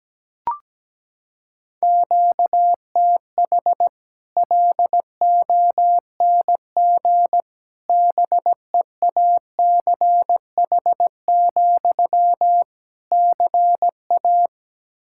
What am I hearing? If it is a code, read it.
Morse code, heard as a keyed tone, about 17 wpm, 700 Hz. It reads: QTH LONG BEACH, CA
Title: Practice QSO - Exchange 1 - Beginning - 17wpm